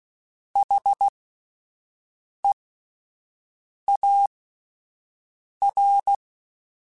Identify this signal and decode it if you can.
Morse code, heard as a keyed tone, about 16 wpm, spaced as for 5 wpm, 800 Hz. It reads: HEAR
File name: Alphabetical / Alphabetical mp3s / Hs @5